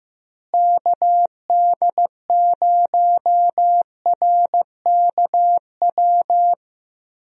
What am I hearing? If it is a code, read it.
Morse code, heard as a keyed tone, about 15 wpm, 700 Hz. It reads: KD0RKW
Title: USA Call Signs - 6 Characters - Repeated 3x - 15wpm